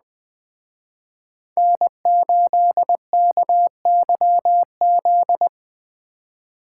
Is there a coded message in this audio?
Yes: Morse code from 1.6 to 5.5 s, heard as a keyed tone, about 22 wpm, 700 Hz. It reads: N8KYZ